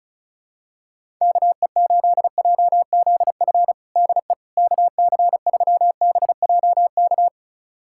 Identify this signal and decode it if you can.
Morse code, heard as a keyed tone, about 34 wpm, 700 Hz. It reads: KE8JZF DE KC3BJK
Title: USA Callsign DE Callsign - 6 Characters - 35wpm